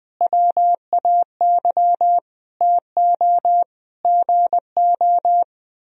Morse code, heard as a keyed tone, about 21 wpm, 700 Hz. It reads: WAY TO GO
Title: Speed-Racing - Sets of 5 Words 20wpm